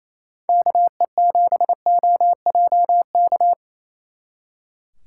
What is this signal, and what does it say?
Morse code, heard as a keyed tone, about 33 wpm, 700 Hz. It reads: KE7OJK
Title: Speed Racing - USA 2x3 Call Signs 28wpm